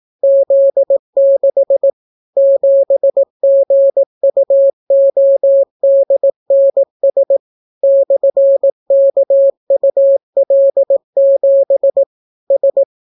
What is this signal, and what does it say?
Morse code, heard as a keyed tone, about 18 wpm, 550 Hz. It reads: Z6 7GUODNS /KUL7 S